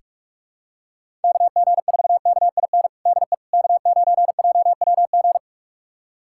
Morse code, heard as a keyed tone, about 46 wpm, 700 Hz. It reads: KG4KIN DE K9JWG
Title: USA Callsign DE Callsign 45wpm